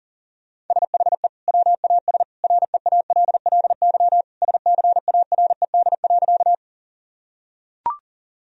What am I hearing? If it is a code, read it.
Morse code, heard as a keyed tone, about 37 wpm, 700 Hz. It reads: SHE WAS REALLY SCARED.
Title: Sentences from Top 1000 Words - Review Part 4 - 40wpm